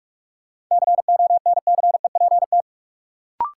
Morse code, heard as a keyed tone, about 46 wpm, 700 Hz. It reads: CONCEPT
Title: Sentences from Top 1100 Words - New Word Drill Alt - 45wpm